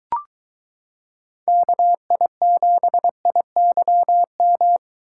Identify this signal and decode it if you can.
Morse code, heard as a keyed tone, about 22 wpm, 700 Hz. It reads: KI7IYM